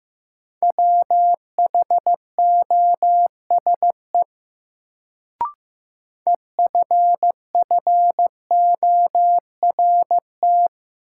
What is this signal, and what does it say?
Morse code, heard as a keyed tone, about 16 wpm, 700 Hz. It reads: WHOSE EFFORT
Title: Sentences from Top 1200 Words - New Word Drill Alt - 15wpm